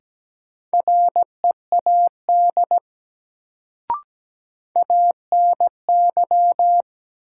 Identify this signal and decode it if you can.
Morse code, heard as a keyed tone, about 17 wpm, 700 Hz. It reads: READ ANY
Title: Sentences from Top 200 Words - New Word Drill - 17wpm